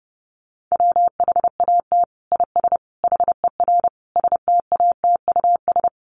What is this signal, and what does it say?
Morse code, heard as a keyed tone, about 28 wpm, 700 Hz. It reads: WHAT IS HER STATUS
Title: Sentences from Top 1400 Words 30wpm